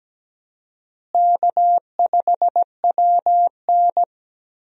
Morse code, heard as a keyed tone, about 17 wpm, 700 Hz. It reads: K5WN
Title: USA Call Signs - 4 Characters - Repeated 3x - 17wpm